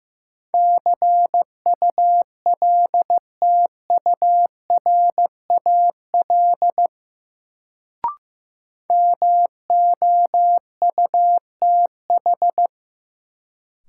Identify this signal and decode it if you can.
Morse code, heard as a keyed tone, about 15 wpm, 700 Hz. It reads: CULTURAL MOUTH
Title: Sentences from Top 1500 Words - New Word Drill Alt - 15wpm